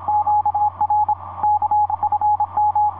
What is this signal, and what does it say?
Morse code, heard as a keyed tone, about 27 wpm, 840 Hz, with radio noise. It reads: QR K FM
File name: 5i3a ea3wai etc 30m cw 09feb2010 1915